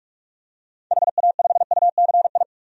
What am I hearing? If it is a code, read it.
Morse code, heard as a keyed tone, about 46 wpm, 700 Hz. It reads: HA5UCI
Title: DXCC-100 Call Signs - 6 Characters - 45wpm